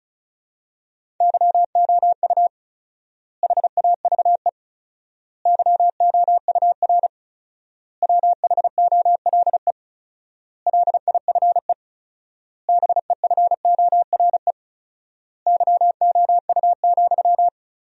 Morse code, heard as a keyed tone, about 36 wpm, 700 Hz. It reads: YOU HAVE YOUR WHOLE LIFE BEFORE YOU,
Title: 4x Word-Spacing - Sentences from Top 400 Words - 35wpm